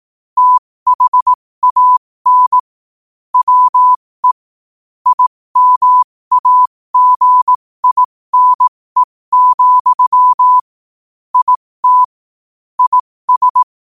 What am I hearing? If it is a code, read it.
Morse code, heard as a keyed tone, about 18 wpm, spaced as for 15 wpm, 1000 Hz. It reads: THAN WE IMAGINE, IT IS
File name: qotdfiles / 15wpm